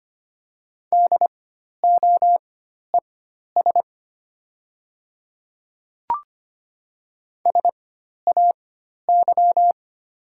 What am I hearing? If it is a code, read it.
Morse code, heard as a keyed tone, about 25 wpm, spaced as for 12 wpm, 700 Hz. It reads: DOES SAY
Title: 4x Character Spacing - Top 200 Words - 25wpm